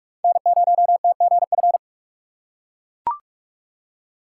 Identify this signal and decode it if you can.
Morse code, heard as a keyed tone, about 43 wpm, 700 Hz. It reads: N0TGF
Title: USA Call Signs - 5 Characters - 3x Character Spacing - 45wpm